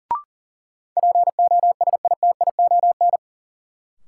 Morse code, heard as a keyed tone, about 40 wpm, 700 Hz. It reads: POSITION
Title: Sentences from Top 900 Words - New Word Drill - 40wpm